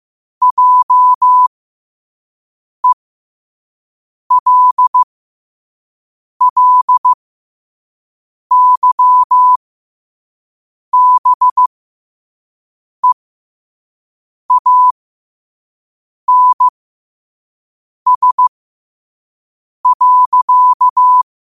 Morse code, heard as a keyed tone, about 15 wpm, spaced as for 5 wpm, 1000 Hz. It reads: JELLYBEANS.
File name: qotdfiles / 5wpm